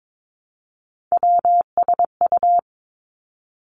Morse code, heard as a keyed tone, about 22 wpm, 700 Hz. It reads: WSU
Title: Three Challenging Letter-Number Combinations 22wpm